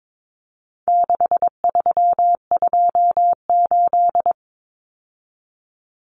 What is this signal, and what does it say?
Morse code, heard as a keyed tone, about 22 wpm, 700 Hz. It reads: 6328
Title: Common Cut Numbers - 4-Digit 22wpm